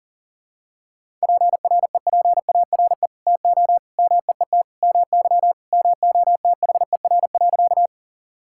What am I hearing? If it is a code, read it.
Morse code, heard as a keyed tone, about 37 wpm, 700 Hz. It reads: PREPARE TO MEET MY MOTHER.
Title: Sentences from Top 1000 Words - Review Part 6 - 40wpm